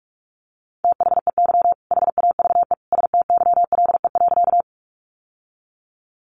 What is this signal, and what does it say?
Morse code, heard as a keyed tone, about 44 wpm, 700 Hz. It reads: THEY HAVE STYLE.
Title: Sentences from Top 800 Words 45wpm